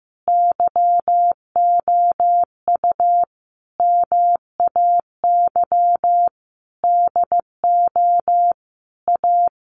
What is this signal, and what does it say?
Morse code, heard as a keyed tone, about 15 wpm, 700 Hz. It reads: YOU MAY DO A
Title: Sentences from Top 900 Words 15wpm